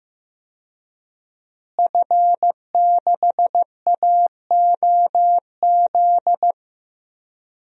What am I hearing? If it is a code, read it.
Morse code, heard as a keyed tone, about 15 wpm, 700 Hz. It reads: F6AOZ
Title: DXCC-100 Call Signs - 5 Characters - Repeated 3x - 15wpm